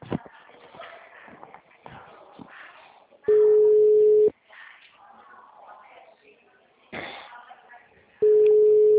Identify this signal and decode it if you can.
Ringback tone (Europe, 425 Hz), with speech.